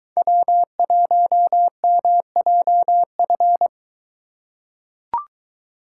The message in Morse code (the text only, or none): W1MJF